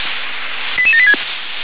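Five-tone selective call (ZVEI), heard as quick runs of five tones, tones of about 70 ms. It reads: 88786
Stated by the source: Received on an SDR; ZVEI five-tone selective call.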